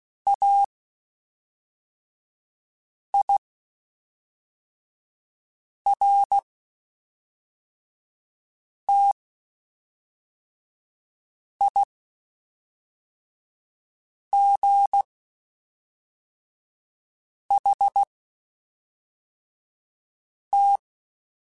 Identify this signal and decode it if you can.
Morse code, heard as a keyed tone, about 16 wpm, spaced as for 3 wpm, 800 Hz. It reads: AIRTIGHT
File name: Quotes Spaced / Quotes Spaced-mp3s / Airtight sp @5